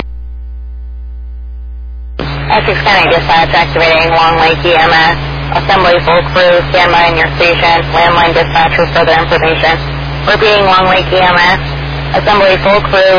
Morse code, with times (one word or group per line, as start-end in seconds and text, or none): none